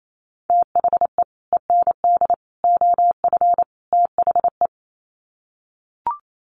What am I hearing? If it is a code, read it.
Morse code, heard as a keyed tone, about 28 wpm, 700 Hz. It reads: THE END OF THE
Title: Sets of 4 Words 28wpm